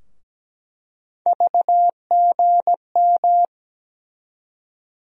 Morse code, heard as a keyed tone, about 17 wpm, 700 Hz. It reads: VGM
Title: USA Call Sign Postfixes - 3 Letters 17wpm